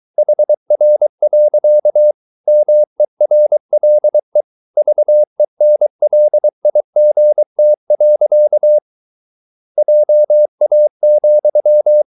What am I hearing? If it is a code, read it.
Morse code, heard as a keyed tone, about 23 wpm, 600 Hz. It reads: HR. MERLE VENLIGT. JA,
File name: Ferie QRS 23wpm